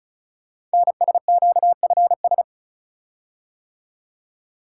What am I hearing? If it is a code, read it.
Morse code, heard as a keyed tone, about 36 wpm, 700 Hz. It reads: NSQFS
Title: Code Groups - 5 Letters - 35wpm